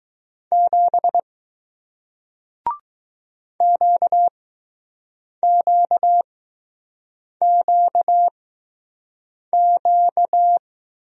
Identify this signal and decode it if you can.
Morse code, heard as a keyed tone, about 20 wpm, spaced as for 6 wpm, 700 Hz. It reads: 7 QQQQ